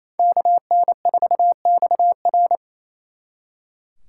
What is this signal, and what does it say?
Morse code, heard as a keyed tone, about 33 wpm, 700 Hz. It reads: KN4XR
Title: Speed Racing - USA 2x2 Call Signs 28wpm